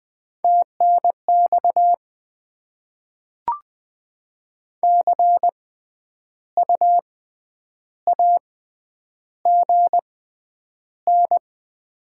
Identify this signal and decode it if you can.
Morse code, heard as a keyed tone, about 20 wpm, spaced as for 9 wpm, 690 Hz. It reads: TNX CU AGN
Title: Top 20 QSO Elements - 6x Character Spacing - 20wpm